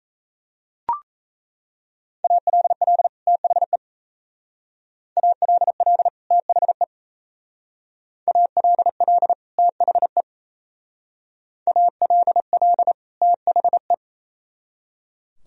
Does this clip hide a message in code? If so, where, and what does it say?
Yes: Morse code from 2.2 to 13.9 s, heard as a keyed tone, about 36 wpm, 700 Hz. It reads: ALL THE ALL THE ALL THE ALL THE